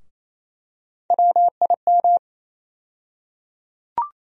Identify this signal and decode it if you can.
Morse code, heard as a keyed tone, about 28 wpm, 700 Hz. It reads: WIM
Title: USA Call Sign Postfixes - 3 Letters 28wpm